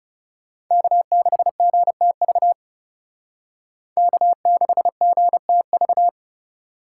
Morse code, heard as a keyed tone, about 40 wpm, 700 Hz. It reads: K6GTV K6GTV